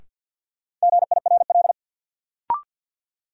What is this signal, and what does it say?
Morse code, heard as a keyed tone, about 53 wpm, 700 Hz. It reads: GIRL
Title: Sentences from Top 400 Words - New Word Drill - 50wpm